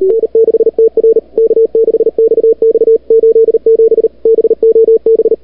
Morse code, heard as a keyed tone, about 40 wpm, 440 Hz. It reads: N6TR K6XX 87 BOB
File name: CWSprint / February 2007 CW Sprint / 01-2007 CW Sprint Hour1